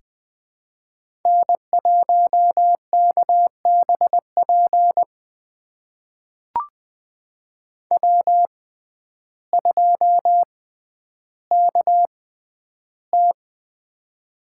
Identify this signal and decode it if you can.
Morse code, heard as a keyed tone, about 20 wpm, spaced as for 11 wpm, 700 Hz. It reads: N1KBP W2KT
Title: USA Call Signs - 6x Character Spacing - 20wpm